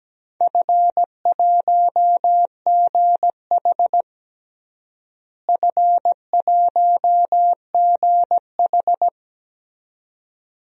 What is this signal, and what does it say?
Morse code, heard as a keyed tone, about 17 wpm, 700 Hz. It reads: F1GH F1GH
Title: DXCC-100 Call Signs - 4 Characters - Repeated 2x - 17wpm